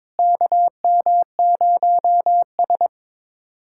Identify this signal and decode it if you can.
Morse code, heard as a keyed tone, about 22 wpm, 700 Hz. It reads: KM0S
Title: USA Callsign DE Callsign - 4 Characters - 22wpm